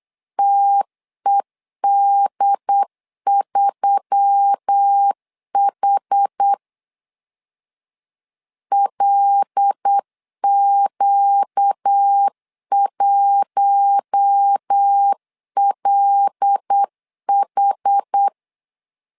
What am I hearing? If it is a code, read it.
Morse code, heard as a keyed tone, about 8 wpm, 800 Hz. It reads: TED3H LQ1LH
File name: LEZIONI / 5 LBVC / 5 L 8wpm 2831